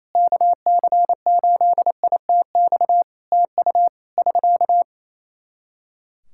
Morse code, heard as a keyed tone, about 28 wpm, 700 Hz. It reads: KC8ITX TU <SK>
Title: Practice QSO - Exchange 4 - Ending - 28wpm